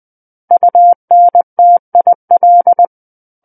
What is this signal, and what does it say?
Morse code, heard as a keyed tone, about 20 wpm, 700 Hz. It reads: UNTIL
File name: 20 WPM